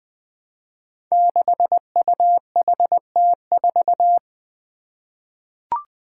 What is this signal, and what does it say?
Morse code, heard as a keyed tone, about 20 wpm, 700 Hz. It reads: 6UHT4
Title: Five Challenging Letter-Number Combinations 20wpm